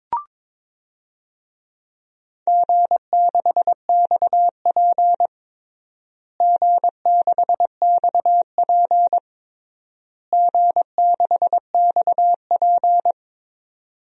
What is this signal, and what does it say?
Morse code, heard as a keyed tone, about 22 wpm, 700 Hz. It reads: G6XP G6XP G6XP
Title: DXCC-100 Call Signs - 4 Characters - Repeated 3x - 22wpm